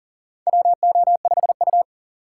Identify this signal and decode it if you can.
Morse code, heard as a keyed tone, about 40 wpm, 700 Hz. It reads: WO5U
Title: USA Call Signs - 4 Characters - 4x Character Spacing - 40wpm